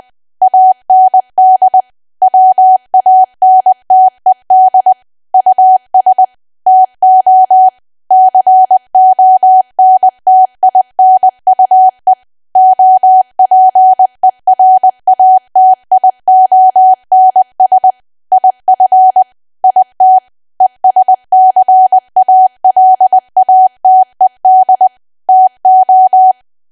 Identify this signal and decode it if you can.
Morse code, heard as a keyed tone, about 20 wpm, 750 Hz. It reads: AND WANTED US TO CONTINUE OPERATIONS IF IT ESCALATED TO
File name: Archive / 20 WPM / 20WPM